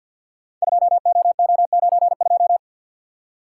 Morse code, heard as a keyed tone, about 52 wpm, 700 Hz. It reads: 2OO92
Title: Code Groups - 5 Letter:Numbers - 50wpm